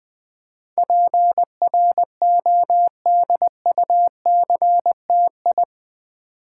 Morse code, heard as a keyed tone, about 20 wpm, 700 Hz. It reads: PRODUCTI
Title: Build A Word - 4 Letters At A Time - 12-Letter Words - 20wpm